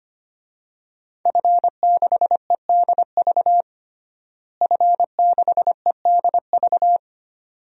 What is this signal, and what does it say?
Morse code, heard as a keyed tone, about 25 wpm, 700 Hz. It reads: F6EDV F6EDV
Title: DXCC-100 Call Signs - 5 Characters - Repeated 2x - 25wpm